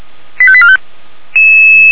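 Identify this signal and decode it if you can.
Five-tone selective call (ZVEI), heard as quick runs of five tones, tones of about 70 ms. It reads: 85845
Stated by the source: ZVEI five-tone selective call received by SDR.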